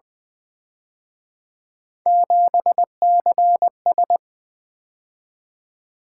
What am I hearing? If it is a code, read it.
Morse code, heard as a keyed tone, about 24 wpm, 700 Hz. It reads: 7CS
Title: Speed-Racing - Three Letter-Number Combinations - 20wpm